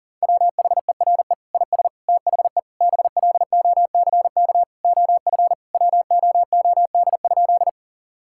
Morse code, heard as a keyed tone, about 40 wpm, 700 Hz. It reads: WHERE IS THE BLOCK OF WOOD?
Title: Sentences from Top 1600 Words 40wpm